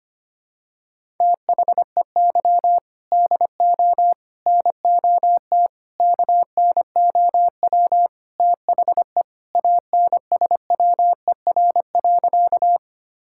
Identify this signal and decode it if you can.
Morse code, heard as a keyed tone, about 25 wpm, 700 Hz. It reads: THEY DO NOT KNOW THE ANSWER.
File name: practice / 25wpm / Sentences with 4-Word Phrases - 25wpm